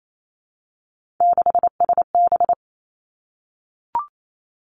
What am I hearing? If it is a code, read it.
Morse code, heard as a keyed tone, about 28 wpm, 700 Hz. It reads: 6SB
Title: Three Challenging Letter-Number Combinations 28wpm